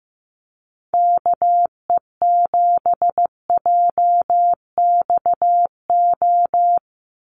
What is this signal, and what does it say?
Morse code, heard as a keyed tone, about 15 wpm, 700 Hz. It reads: KE7JXO
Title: USA Call Signs 15wpm